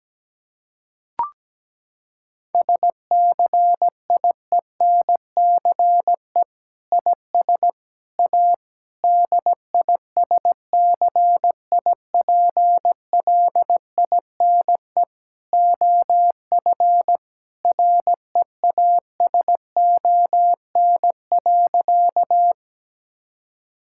Morse code, heard as a keyed tone, about 17 wpm, 700 Hz. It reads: SCIENCE IS A DISCIPLINE OF REASON.
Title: Sentences from Top 1000 Words - Review Part 6 - 17wpm